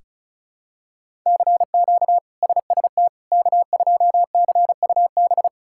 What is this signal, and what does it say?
Morse code, heard as a keyed tone, about 36 wpm, 700 Hz. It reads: CQ SST K2CUB
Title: K1USN SST QSO Practice - 35wpm